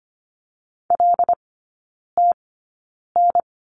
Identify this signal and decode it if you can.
Morse code, heard as a keyed tone, about 25 wpm, spaced as for 9 wpm, 700 Hz. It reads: LTN
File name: course / 25:10wpm / 093 Lesson 16e - Letter c - Three Characters 25wpm